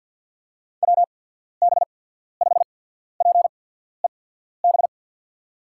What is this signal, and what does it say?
Morse code, heard as a keyed tone, about 53 wpm, spaced as for 14 wpm, 700 Hz. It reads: WB5PEB
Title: USA Call Signs - 6 Characters - 8x Character Spacing - 50wpm